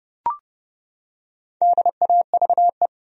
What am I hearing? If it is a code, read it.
Morse code, heard as a keyed tone, about 32 wpm, 700 Hz. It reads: DAVE
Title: USA Common Names - 30wpm